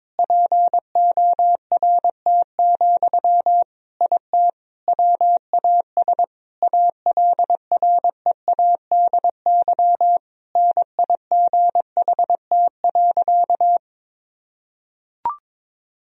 Morse code, heard as a keyed tone, about 21 wpm, 700 Hz. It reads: PORT, IT WAS ALREADY NIGHT.
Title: Sentences from Top 1000 Words - Review Part 1 - 22wpm